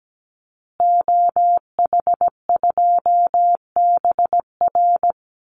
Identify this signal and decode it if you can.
Morse code, heard as a keyed tone, about 17 wpm, 700 Hz. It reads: OH2BR
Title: Call Signs Repeated 3x - 17wpm